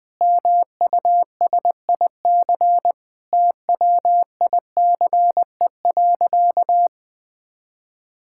Morse code, heard as a keyed tone, about 20 wpm, 700 Hz. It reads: MUSIC TWICE.
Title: Sentences from Top 1600 Words 20wpm